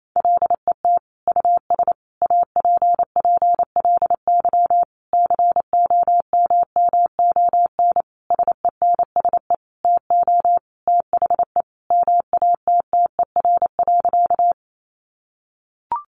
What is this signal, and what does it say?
Morse code, heard as a keyed tone, about 28 wpm, 700 Hz. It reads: LET US APPLY COMMON SENSE TO THE MATTER.
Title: Sentences from Top 1200 Words 28wpm